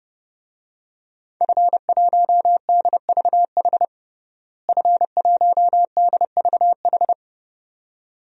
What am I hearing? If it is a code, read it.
Morse code, heard as a keyed tone, about 30 wpm, 700 Hz. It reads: F1DVH F1DVH
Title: DXCC-100 Call Signs - 5 Characters - Repeated 2x - 30wpm